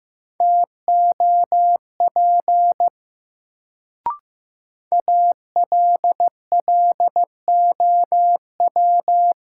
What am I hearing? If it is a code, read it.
Morse code, heard as a keyed tone, about 16 wpm, 700 Hz. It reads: TOP ALLOW
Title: Sentences from Top 700 Words - New Word Drill - 15wpm